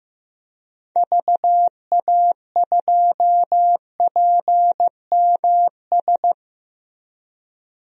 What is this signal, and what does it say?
Morse code, heard as a keyed tone, about 15 wpm, 700 Hz. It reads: VA2PMS
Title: DXCC-100 Call Signs - 6 Characters - Repeated 2x - 15wpm